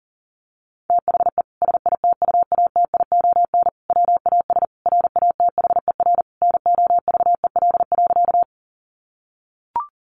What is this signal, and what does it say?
Morse code, heard as a keyed tone, about 40 wpm, 700 Hz. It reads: THE SITUATION WAS RATHER NOVEL.
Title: Sentences from Top 1300 Words 40wpm